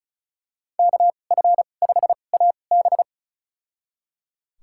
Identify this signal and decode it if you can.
Morse code, heard as a keyed tone, about 34 wpm, spaced as for 25 wpm, 700 Hz. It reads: KF5AB
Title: USA Call Signs - 2x Character Spacing - 35wpm